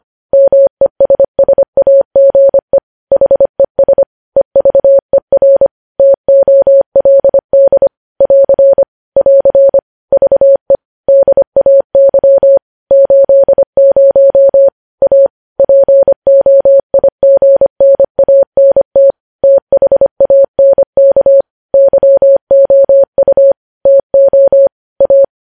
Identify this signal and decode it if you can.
Morse code, heard as a keyed tone, about 25 wpm, 550 Hz. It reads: MESSAGE HES EVER TOLD <AR> <AR> VE DAY 80 A POIGNANT THANK YOU TO A
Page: News Headlines in Morse Code at 25 WPM